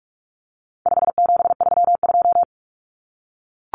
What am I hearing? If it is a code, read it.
Morse code, heard as a keyed tone, about 44 wpm, 700 Hz. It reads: 5732
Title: Numbers - 4-Digit 45wpm